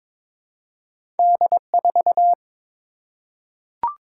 Morse code, heard as a keyed tone, about 22 wpm, 700 Hz. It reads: D4
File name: practice / 22wpm / Two Letter-Number Combinations - 22wpm